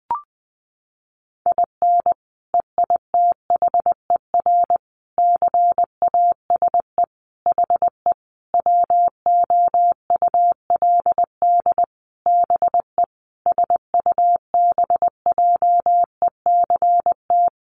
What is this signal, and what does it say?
Morse code, heard as a keyed tone, about 20 wpm, 700 Hz. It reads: IN EITHER CASE HE WOULD BE SUBJECT
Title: Sentences from Top 1300 Words 20wpm